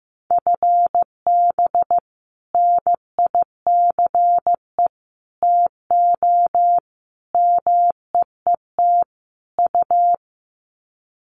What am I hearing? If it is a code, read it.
Morse code, heard as a keyed tone, about 15 wpm, 700 Hz. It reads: FB NICE TO MEET U
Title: Common QSO Phrases 15wpm